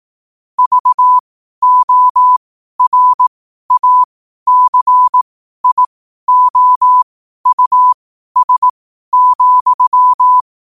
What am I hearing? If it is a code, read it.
Morse code, heard as a keyed tone, about 18 wpm, spaced as for 12 wpm, 1000 Hz. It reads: VORACIOUS,
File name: 12wpm